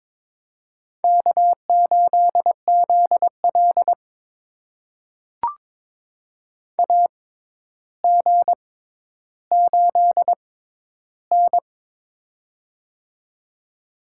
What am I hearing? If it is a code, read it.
Morse code, heard as a keyed tone, about 22 wpm, spaced as for 11 wpm, 700 Hz. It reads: K8ZL AG8N
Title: USA Call Signs - 4 Characters - 6x Character Spacing - 22wpm